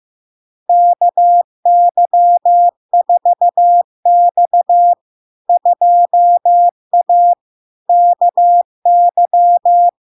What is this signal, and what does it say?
Morse code, heard as a keyed tone, about 15 wpm, 700 Hz. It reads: KY4X 2A KY
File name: LARK-FD-2022-Log-in-CW-15wpm-700-Hz